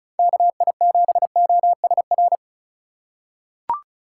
Morse code, heard as a keyed tone, about 36 wpm, 700 Hz. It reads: KI7OSR
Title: USA Call Signs - 6 Characters - 2x Character Spacing - 35wpm